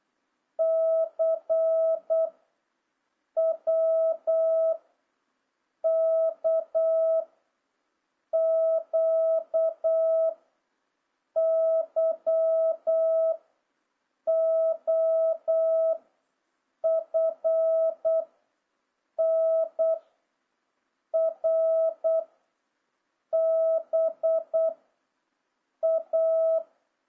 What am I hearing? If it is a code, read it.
Morse code, heard as a keyed tone, about 8 wpm, 650 Hz. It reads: C W K Q Y O F N R B A
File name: jh8cbh / 2026cw